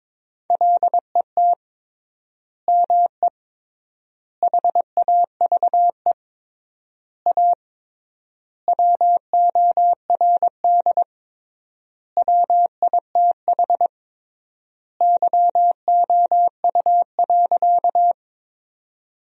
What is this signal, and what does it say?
Morse code, heard as a keyed tone, about 22 wpm, 700 Hz. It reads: LET ME HAVE A WORD WITH YOU.
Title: 3x Word-Spacing - Sentences from Top 300 Words - 22wpm